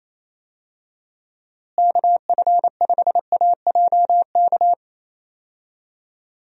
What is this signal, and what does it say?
Morse code, heard as a keyed tone, about 27 wpm, 700 Hz. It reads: KF5AJK
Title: USA Call Signs - 6 Characters - Repeated 2x - 28wpm